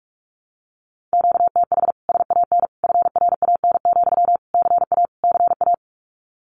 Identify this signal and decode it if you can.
Morse code, heard as a keyed tone, about 44 wpm, 700 Hz. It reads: QTH SAN FRAN, CA CA
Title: Common QSO Phrases Encore 45wpm